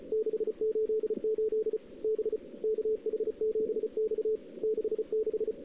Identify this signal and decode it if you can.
Morse code, heard as a keyed tone, about 35 wpm, 410 Hz. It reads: 688 B KH7X 66
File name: CW / 2008-ss-cw-k5tr-op-n5rz -06:00